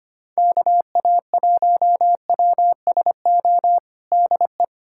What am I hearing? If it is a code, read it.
Morse code, heard as a keyed tone, about 25 wpm, 700 Hz. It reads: KA1WSO DE
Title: USA Callsign DE Callsign - 6 Characters - 25wpm